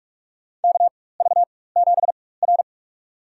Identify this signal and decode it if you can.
Morse code, heard as a keyed tone, about 44 wpm, spaced as for 21 wpm, 700 Hz. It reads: KV7R
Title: USA Call Signs - 4 Characters - 4x Character Spacing - 45wpm